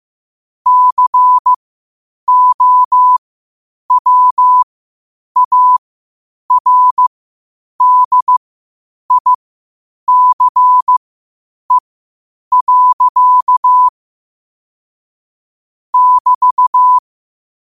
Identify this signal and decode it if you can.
Morse code, heard as a keyed tone, about 15 wpm, spaced as for 8 wpm, 1000 Hz. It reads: COWARDICE. <BT>
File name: qotdfiles / 8wpm